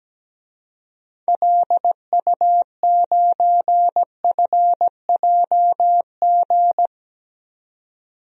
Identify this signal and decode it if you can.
Morse code, heard as a keyed tone, about 17 wpm, 700 Hz. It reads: LU9FJG
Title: DXCC-100 Call Signs - 6 Characters - 17wpm